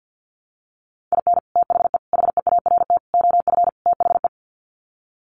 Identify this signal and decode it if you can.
Morse code, heard as a keyed tone, about 50 wpm, 700 Hz. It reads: IN THE HEART OF THE
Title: Sets of 5 Words 50wpm